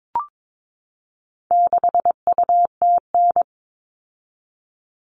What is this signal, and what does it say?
Morse code, heard as a keyed tone, about 22 wpm, 700 Hz. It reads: 6UTN
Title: All Cut Numbers - 4-Digit 22wpm